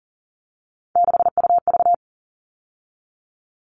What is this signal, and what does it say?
Morse code, heard as a keyed tone, about 40 wpm, 700 Hz. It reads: 6UV
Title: All Cut Numbers - 3-Digit 40wpm